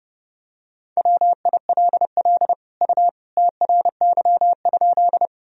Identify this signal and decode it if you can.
Morse code, heard as a keyed tone, about 30 wpm, 700 Hz. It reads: WILL U TRY?
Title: Sentences from Top 200 Words and Abbreviations 30wpm